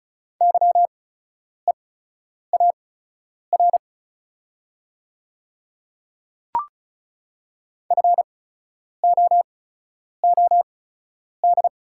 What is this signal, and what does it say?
Morse code, heard as a keyed tone, about 36 wpm, spaced as for 10 wpm, 700 Hz. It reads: YEAR FOOD
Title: 8x Character Spacing - Top 300 Words - 35wpm